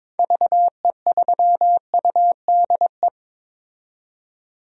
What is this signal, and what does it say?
Morse code, heard as a keyed tone, about 22 wpm, 700 Hz. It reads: VE3UDE
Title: DXCC-100 Call Signs - 6 Characters - 22wpm